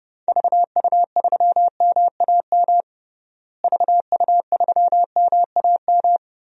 Morse code, heard as a keyed tone, about 32 wpm, 700 Hz. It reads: VU3MAM VU3MAM
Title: DXCC-100 Call Signs - 6 Characters - Repeated 2x - 30wpm